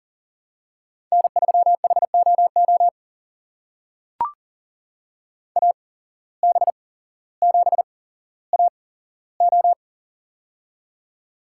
Morse code, heard as a keyed tone, about 40 wpm, spaced as for 17 wpm, 700 Hz. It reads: N3HOO AB7AO
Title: USA Call Signs - 5 Characters - 8x Character Spacing - 40wpm